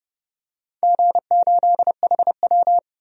Morse code, heard as a keyed tone, about 30 wpm, 700 Hz. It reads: G8HW
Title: DXCC-100 Call Signs - 4 Characters - 30wpm